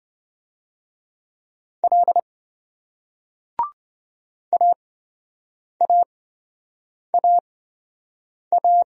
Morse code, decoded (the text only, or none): L AAAA